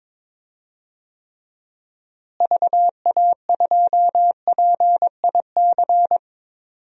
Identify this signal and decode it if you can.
Morse code, heard as a keyed tone, about 22 wpm, 700 Hz. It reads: VA2PIC
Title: DXCC-100 Call Signs - 6 Characters - Repeated 3x - 22wpm